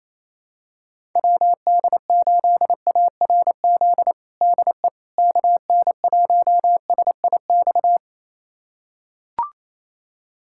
Morse code, heard as a keyed tone, about 28 wpm, 700 Hz. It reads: WD8ARZ DE KN1SIX
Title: USA Callsign DE Callsign - 6 Characters - 28wpm